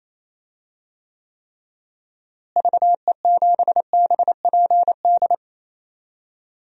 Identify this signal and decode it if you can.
Morse code, heard as a keyed tone, about 28 wpm, 700 Hz. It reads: VE7BPD
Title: DXCC-100 Call Signs - 6 Characters - Repeated 3x - 28wpm